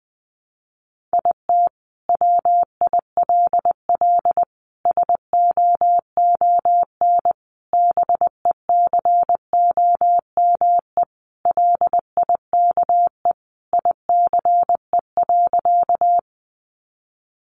Morse code, heard as a keyed tone, about 20 wpm, 700 Hz. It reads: IT WILL SOON BECOME LIKE ICE.
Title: Sentences from Top 1000 Words 20wpm